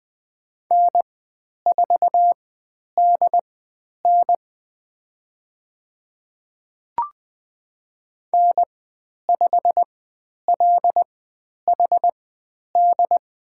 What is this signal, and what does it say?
Morse code, heard as a keyed tone, about 20 wpm, spaced as for 10 wpm, 700 Hz. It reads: N4DN N5LHD
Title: 109 Lesson 18g - Letter m - Call Signs 20wpm